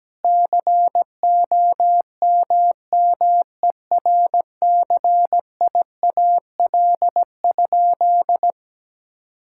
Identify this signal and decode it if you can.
Morse code, heard as a keyed tone, about 17 wpm, 700 Hz. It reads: COMMERCIAL?
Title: Sentences from Top 1000 Words - Review Part 5 - 17wpm